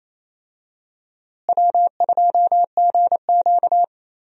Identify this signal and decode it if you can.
Morse code, heard as a keyed tone, about 28 wpm, 700 Hz. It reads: W2GQ
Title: USA Call Signs - 4 Characters - Repeated 3x - 28wpm